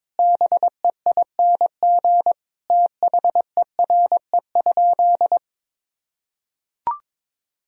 Morse code, heard as a keyed tone, about 22 wpm, 700 Hz. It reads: BEING THERE?
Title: Sentences from Top 1000 Words - Review Part 2 - 22wpm